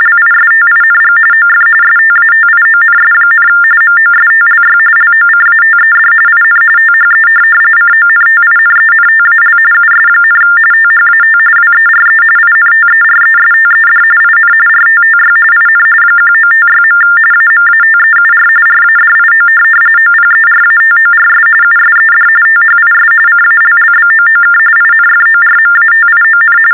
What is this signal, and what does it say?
TTY/TDD text telephone tones, about 45.45 baud. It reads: LISTING OF STATIONS, MOSTLY LOCATED IN THE CONTINENTAL US, THAT WERE INTERESTED IN RTTY IN 1956. AMATEUR RADIO OPERATORS USED THIS CALLBOOK INFORMATION
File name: teletype-RTTY-45